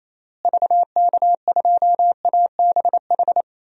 Morse code, heard as a keyed tone, about 28 wpm, 700 Hz. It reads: VK2ABH
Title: DXCC-100 Call Signs - 6 Characters - Repeated 2x - 28wpm